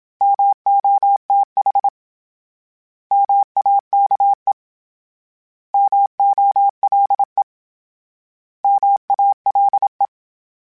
Morse code, heard as a keyed tone, about 27 wpm, 800 Hz. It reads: MOTH MAKE MOLE MALE
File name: Misc / Misc-mp3s / Misc @25